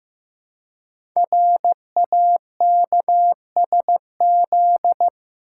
Morse code, heard as a keyed tone, about 15 wpm, 700 Hz. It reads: RAKSZ
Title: Code Groups - 5 Letter:Numbers - 15wpm